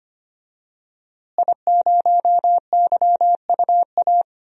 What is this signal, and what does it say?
Morse code, heard as a keyed tone, about 25 wpm, 700 Hz. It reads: I0YUA
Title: DXCC-100 Call Signs - 5 Characters - Repeated 3x - 25wpm